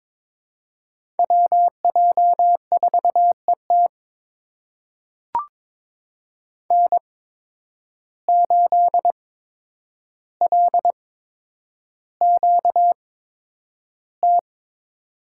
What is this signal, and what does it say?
Morse code, heard as a keyed tone, about 22 wpm, spaced as for 9 wpm, 700 Hz. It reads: WJ4ET N8LQT
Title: USA Call Signs - 5 Characters - 8x Character Spacing - 22wpm